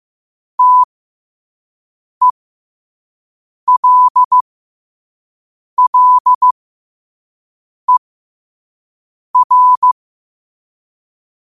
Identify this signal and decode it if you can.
Morse code, heard as a keyed tone, about 15 wpm, spaced as for 5 wpm, 1000 Hz. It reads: TELLER
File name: qotdfiles / 5wpm